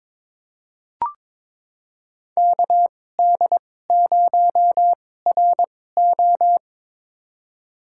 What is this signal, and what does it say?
Morse code, heard as a keyed tone, about 22 wpm, spaced as for 16 wpm, 700 Hz. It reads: KD0RO
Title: USA Call Signs - 5 Characters - 2x Character Spacing - 22wpm